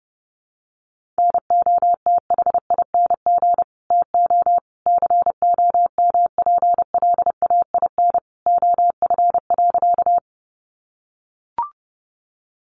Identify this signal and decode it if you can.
Morse code, heard as a keyed tone, about 30 wpm, 700 Hz. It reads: NOTHING TO COMPLAIN OF.
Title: Sentences from Top 1300 Words 30wpm